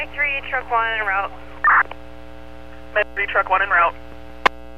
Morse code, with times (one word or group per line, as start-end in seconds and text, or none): none